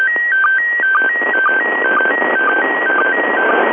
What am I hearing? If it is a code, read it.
SSTV (slow-scan television), PD 120.
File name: dc9dd / isssstv